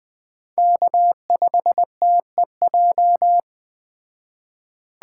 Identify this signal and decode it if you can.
Morse code, heard as a keyed tone, about 20 wpm, 700 Hz. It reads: K5TEJ